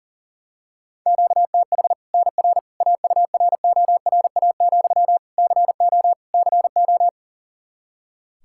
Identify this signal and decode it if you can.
Morse code, heard as a keyed tone, about 40 wpm, 700 Hz. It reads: QTH NR AURORA, CO CO
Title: Practice QSO - Exchange 1 - Beginning - 40wpm